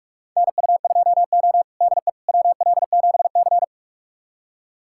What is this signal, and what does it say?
Morse code, heard as a keyed tone, about 46 wpm, 700 Hz. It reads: NU2O DE WL7C
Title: USA Callsign DE Callsign - 4 Characters - 45wpm